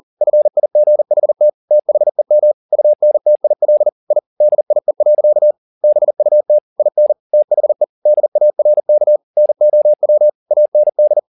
Morse code, heard as a keyed tone, about 40 wpm, 600 Hz. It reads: FIGHT THEM UNTIL I DIE. BUT IN THE DARK NOW AND
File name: the old man and the sea 2 40wpm